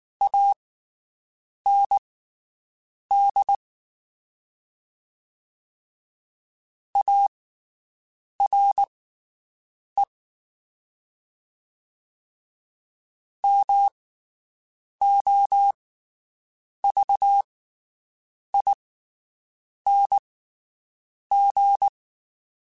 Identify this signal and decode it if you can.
Morse code, heard as a keyed tone, about 19 wpm, spaced as for 7 wpm, 800 Hz. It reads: AND ARE MOVING
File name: (spaced) / QSOs Spaced-mp3s / WD0BGP 1 @18 sp